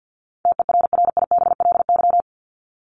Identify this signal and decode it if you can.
Morse code, heard as a keyed tone, about 50 wpm, 700 Hz. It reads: TERRIBLY
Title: Top 1751-2000 Words 50wpm